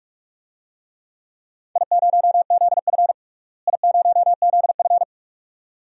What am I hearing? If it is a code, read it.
Morse code, heard as a keyed tone, about 41 wpm, 700 Hz. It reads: I0ZF I0ZF
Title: DXCC-100 Call Signs - 4 Characters - Repeated 2x - 45wpm